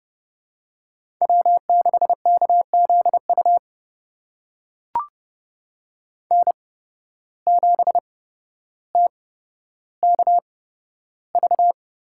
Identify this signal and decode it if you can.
Morse code, heard as a keyed tone, about 30 wpm, spaced as for 13 wpm, 700 Hz. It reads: W6KZU N7TKV
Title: USA Call Signs - 5 Characters - 8x Character Spacing - 30wpm